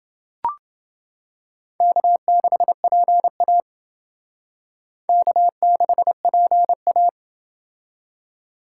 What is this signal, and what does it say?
Morse code, heard as a keyed tone, about 28 wpm, 700 Hz. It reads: K6PA K6PA